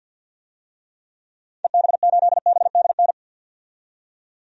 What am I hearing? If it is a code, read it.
Morse code, heard as a keyed tone, about 47 wpm, 700 Hz. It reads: EB8BDN
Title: DXCC-100 Call Signs - 6 Characters - 50wpm